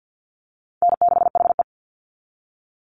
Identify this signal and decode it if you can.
Morse code, heard as a keyed tone, about 50 wpm, 700 Hz. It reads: N6HE
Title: Easy Call Signs - 50wpm